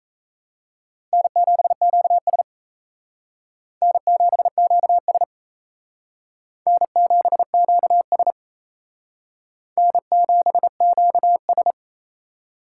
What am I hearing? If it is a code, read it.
Morse code, heard as a keyed tone, about 35 wpm, 700 Hz. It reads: N7QS N7QS N7QS N7QS
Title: Speed Racing - USA 1x2 Call Signs 28wpm